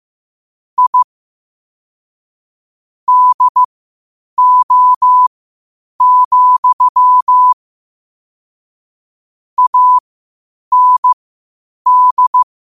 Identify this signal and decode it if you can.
Morse code, heard as a keyed tone, about 15 wpm, spaced as for 8 wpm, 1000 Hz. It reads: I DO, AND
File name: qotdfiles / 8wpm